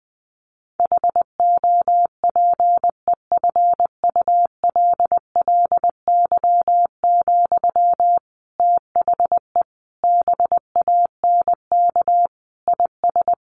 Morse code, heard as a keyed tone, about 20 wpm, 700 Hz. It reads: HOPEFULLY, THE BANK IS
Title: Sentences from Top 1500 Words 20wpm